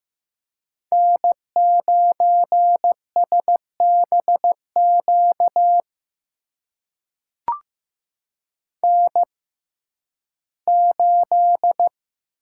Morse code, heard as a keyed tone, about 15 wpm, spaced as for 11 wpm, 700 Hz. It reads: N9SBQ N8
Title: USA Call Signs - 5 Characters - 6x Character Spacing - 15wpm